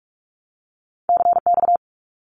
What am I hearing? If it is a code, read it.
Morse code, heard as a keyed tone, about 46 wpm, 700 Hz. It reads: CX
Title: Abbreviations 45wpm